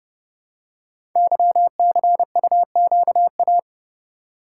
Morse code, heard as a keyed tone, about 30 wpm, 700 Hz. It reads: YCUQA
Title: Code Groups - 5 Letters - 30wpm